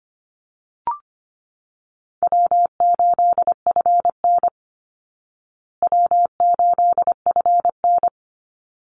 Morse code, heard as a keyed tone, about 25 wpm, 700 Hz. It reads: W8FN W8FN
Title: Easy Call Signs Repeated 2x - 25wpm